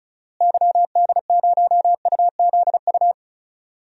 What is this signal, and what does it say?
Morse code, heard as a keyed tone, about 36 wpm, 700 Hz. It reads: YD0UZU
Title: DXCC-100 Call Signs - 6 Characters - Repeated 3x - 35wpm